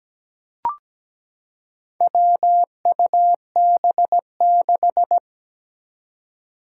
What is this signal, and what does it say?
Morse code, heard as a keyed tone, about 17 wpm, 700 Hz. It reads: WUB6
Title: Four Challenging Letter-Number Combinations 17wpm